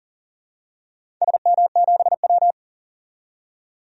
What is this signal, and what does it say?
Morse code, heard as a keyed tone, about 40 wpm, 700 Hz. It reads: SM7W
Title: DXCC-100 Call Signs - 4 Characters - 40wpm